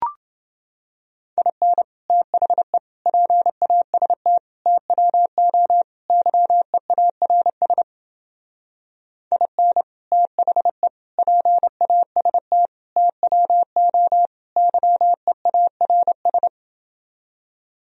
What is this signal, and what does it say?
Morse code, heard as a keyed tone, about 28 wpm, 700 Hz. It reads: IN THE PAST TWO YEARS IN THE PAST TWO YEARS